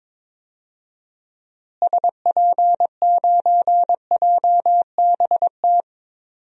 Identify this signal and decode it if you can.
Morse code, heard as a keyed tone, about 22 wpm, 700 Hz. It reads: SP9JBT
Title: DXCC-100 Call Signs - 6 Characters - Repeated 3x - 22wpm